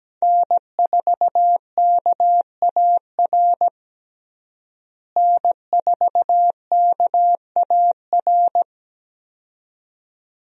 Morse code, heard as a keyed tone, about 17 wpm, 700 Hz. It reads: N4KAR N4KAR
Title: USA Call Signs - 5 Characters - Repeated 2x - 17wpm